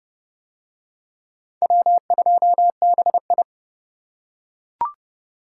Morse code, heard as a keyed tone, about 28 wpm, 700 Hz. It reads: W2BI